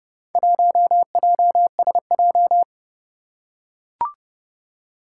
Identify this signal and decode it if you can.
Morse code, heard as a keyed tone, about 29 wpm, 690 Hz. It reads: 1JSJ
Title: Four Challenging Letter-Number Combinations 30wpm